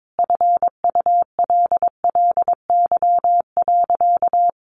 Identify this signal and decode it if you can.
Morse code, heard as a keyed tone, about 22 wpm, 700 Hz. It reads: FULLY.
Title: Sentences from Top 1300 Words 22wpm